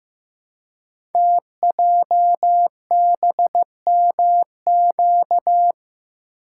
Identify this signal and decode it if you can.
Morse code, heard as a keyed tone, about 15 wpm, 700 Hz. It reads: TJBMQ
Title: Code Groups - 5 Letters - 15wpm